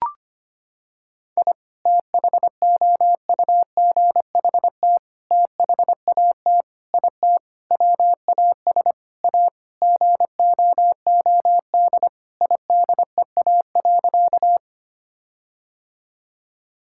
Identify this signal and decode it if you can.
Morse code, heard as a keyed tone, about 25 wpm, 700 Hz. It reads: I THOUGHT THAT IT WAS A GOOD IDEA.
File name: practice / 25wpm / Sentences with 4-Word Phrases - 25wpm